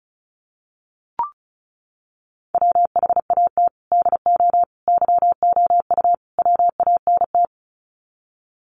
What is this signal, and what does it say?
Morse code, heard as a keyed tone, about 34 wpm, 700 Hz. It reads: WHAT DO YOU WANT
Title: Sets of 4 Words 35wpm